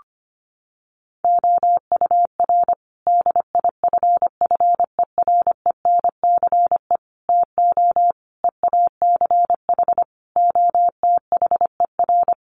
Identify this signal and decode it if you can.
Morse code, heard as a keyed tone, about 25 wpm, 700 Hz. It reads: OUR DIFFERENCE TO EACH OTHER
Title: Sentences from Top 1500 Words 25wpm